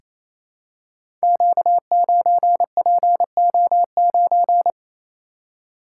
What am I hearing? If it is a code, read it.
Morse code, heard as a keyed tone, about 27 wpm, 700 Hz. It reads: Q9PO9
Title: Code Groups - 5 Letter:Numbers - 28wpm